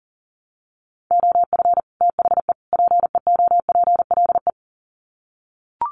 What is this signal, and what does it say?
Morse code, heard as a keyed tone, about 40 wpm, 700 Hz. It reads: OF THE PEOPLE
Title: Sets of 3 Words 40wpm